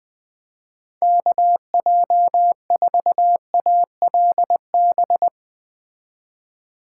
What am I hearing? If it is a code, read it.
Morse code, heard as a keyed tone, about 21 wpm, 700 Hz. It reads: KJ4ALB